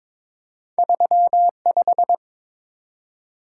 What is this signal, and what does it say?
Morse code, heard as a keyed tone, about 22 wpm, 700 Hz. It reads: 35
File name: practice / 22wpm / Two Letter-Number Combinations - 22wpm